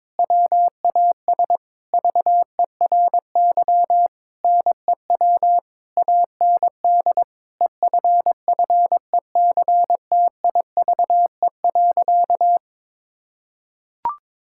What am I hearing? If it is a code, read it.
Morse code, heard as a keyed tone, about 22 wpm, 700 Hz. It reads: WAS VERY NEW AND EFFECTIVE.
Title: Sentences from Top 1000 Words - Review Part 6 - 22wpm